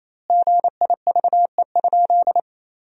Morse code, heard as a keyed tone, about 28 wpm, 700 Hz. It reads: GIVE?
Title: 2x Word-Spacing - Sentences from Top 200 Words - 28wpm